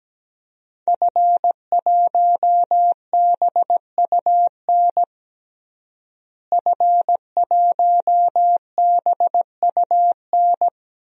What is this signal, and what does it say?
Morse code, heard as a keyed tone, about 17 wpm, 700 Hz. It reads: F1BUN F1BUN
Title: DXCC-100 Call Signs - 5 Characters - Repeated 3x - 17wpm